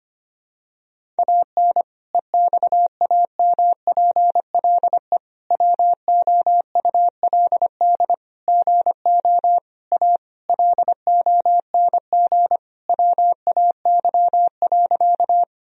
Morse code, heard as a keyed tone, about 25 wpm, 700 Hz. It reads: AN EXAMPLE WOULD GO A LONG WAY.
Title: Sentences from Top 1000 Words - Review Part 1 - 25wpm